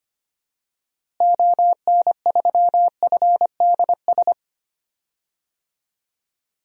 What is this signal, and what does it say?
Morse code, heard as a keyed tone, about 25 wpm, 700 Hz. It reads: ON3FDS
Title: DXCC-100 Call Signs - 6 Characters - Repeated 2x - 25wpm